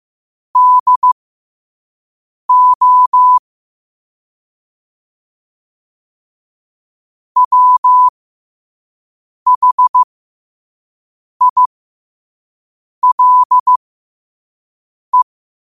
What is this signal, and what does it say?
Morse code, heard as a keyed tone, about 15 wpm, spaced as for 5 wpm, 1000 Hz. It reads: DO WHILE
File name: qotdfiles / 5wpm